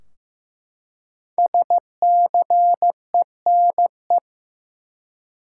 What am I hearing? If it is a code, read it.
Morse code, heard as a keyed tone, about 15 wpm, 700 Hz. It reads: SCENE
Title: Five Letter Words - 15wpm